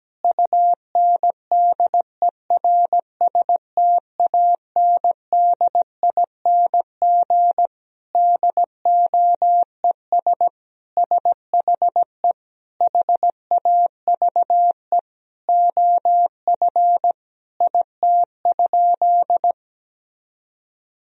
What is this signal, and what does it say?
Morse code, heard as a keyed tone, about 17 wpm, 700 Hz. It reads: UNDERSTANDING DOES SHE HAVE OF IT?
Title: Sentences from Top 1000 Words - Review Part 3 - 17wpm